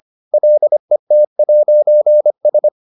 Morse code, heard as a keyed tone, about 25 wpm, 600 Hz. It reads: LET'S